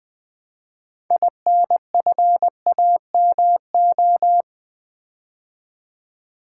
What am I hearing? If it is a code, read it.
Morse code, heard as a keyed tone, about 20 wpm, 690 Hz. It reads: INFAMO
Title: Build A Word - 3 Letters At A Time - 8-Letter Words - 20wpm